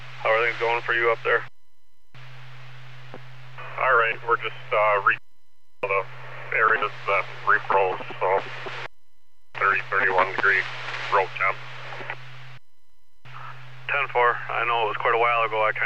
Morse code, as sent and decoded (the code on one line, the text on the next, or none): none